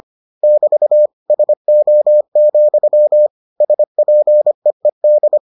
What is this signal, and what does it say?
Morse code, heard as a keyed tone, about 25 wpm, 600 Hz. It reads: <BT> SO, SPEED